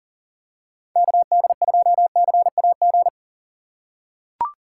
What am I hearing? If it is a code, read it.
Morse code, heard as a keyed tone, about 39 wpm, 700 Hz. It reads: KD2CAG
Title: USA Call Signs - 6x Character Spacing - 40wpm